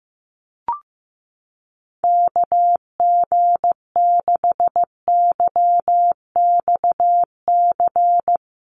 Morse code, heard as a keyed tone, about 15 wpm, 700 Hz. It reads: KG6YXC
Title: USA Call Signs Repeated 2x 15wpm